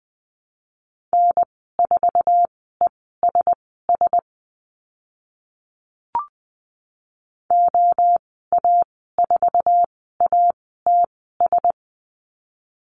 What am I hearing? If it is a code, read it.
Morse code, heard as a keyed tone, about 20 wpm, spaced as for 15 wpm, 700 Hz. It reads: N4ESS OA4ATS
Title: 055 Lesson 10g - Letter r - Call Signs 20wpm